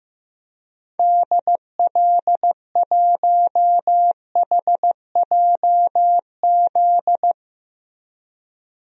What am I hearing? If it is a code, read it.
Morse code, heard as a keyed tone, about 16 wpm, 700 Hz. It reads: DL1HJZ
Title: DXCC-100 Call Signs - 6 Characters - Repeated 3x - 15wpm